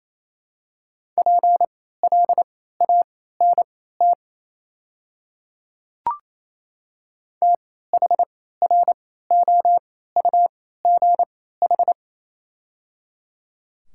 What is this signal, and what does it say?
Morse code, heard as a keyed tone, about 28 wpm, spaced as for 16 wpm, 700 Hz. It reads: PLANT THROUGH
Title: 3x Character Spacing - Top 300 Words - 28wpm